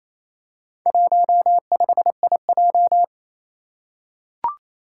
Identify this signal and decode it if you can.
Morse code, heard as a keyed tone, about 29 wpm, 700 Hz. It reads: 15IJ
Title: Four Challenging Letter-Number Combinations 28wpm